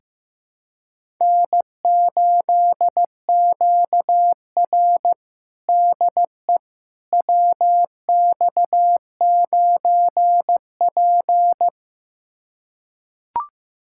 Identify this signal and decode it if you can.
Morse code, heard as a keyed tone, about 16 wpm, 700 Hz. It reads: N8QR DE WX9P
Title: USA Callsign DE Callsign - 4 Characters - 15wpm